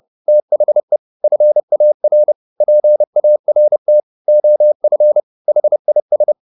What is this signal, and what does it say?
Morse code, heard as a keyed tone, about 30 wpm, 600 Hz. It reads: THE FAR PART OF HIS